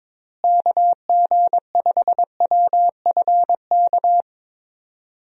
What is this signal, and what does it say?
Morse code, heard as a keyed tone, about 21 wpm, 700 Hz. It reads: KG5WFK
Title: USA Call Signs - 6 Characters - Repeated 2x - 22wpm